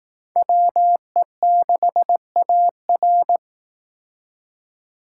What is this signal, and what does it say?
Morse code, heard as a keyed tone, about 18 wpm, 700 Hz. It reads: WE6AR
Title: Speed Racing - USA 2x2 Call Signs 15wpm